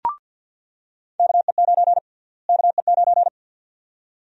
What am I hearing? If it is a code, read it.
Morse code, heard as a keyed tone, about 48 wpm, 700 Hz. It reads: KE9 KE9
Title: USA Call Sign Prefixes - Repeated 2x - 50wpm